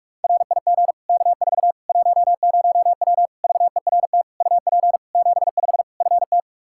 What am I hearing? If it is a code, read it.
Morse code, heard as a keyed tone, about 42 wpm, 700 Hz. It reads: RIG K4 10W VERT UP 75 FT
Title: Practice QSO - Exchange 2 - Info - 45wpm